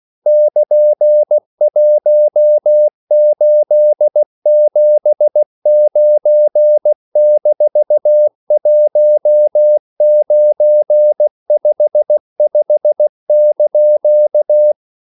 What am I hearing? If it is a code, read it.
Morse code, heard as a keyed tone, about 16 wpm, 600 Hz. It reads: <KN>1879-1955)
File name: CW / Practice / EnglishQuotations0416